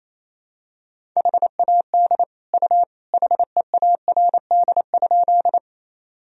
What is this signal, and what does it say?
Morse code, heard as a keyed tone, about 27 wpm, 700 Hz. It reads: HAD U HEARD?
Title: Sentences from Top 400 Words and Abbreviations 28wpm